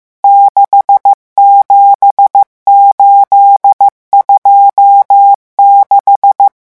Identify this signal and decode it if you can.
Morse code, heard as a keyed tone, about 15 wpm, 800 Hz. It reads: 67826
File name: morse mp3 / revision11 chiffres vitesse 14